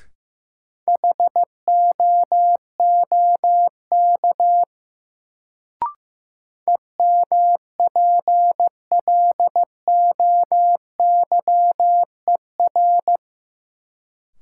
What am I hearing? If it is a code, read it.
Morse code, heard as a keyed tone, about 15 wpm, 700 Hz. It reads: HOOK EMPLOYER
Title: Sentences from Top 1600 Words - New Word Drill - 15wpm